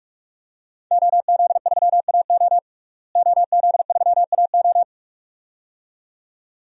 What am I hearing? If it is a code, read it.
Morse code, heard as a keyed tone, about 41 wpm, 700 Hz. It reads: OZ3AO OZ3AO
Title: DXCC-100 Call Signs - 5 Characters - Repeated 2x - 45wpm